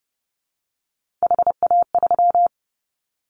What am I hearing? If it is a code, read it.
Morse code, heard as a keyed tone, about 30 wpm, 700 Hz. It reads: HA3
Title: Callsign Prefixes for Top 25 DXCC - Repeated 3x - 30wpm